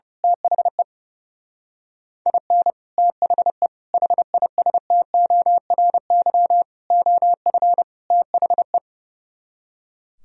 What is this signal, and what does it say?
Morse code, heard as a keyed tone, about 28 wpm, 700 Hz. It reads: THE IN THE HISTORY OF THE
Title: Speed-Racing - Sets of 5 Words 30wpm